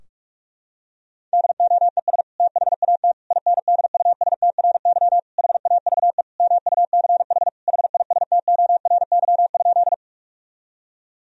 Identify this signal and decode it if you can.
Morse code, heard as a keyed tone, about 46 wpm, 700 Hz. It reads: DOES THAT INDUSTRY HAVE MUCH HISTORY?
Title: Sentences from Top 1000 Words - Review Part 4 - 45wpm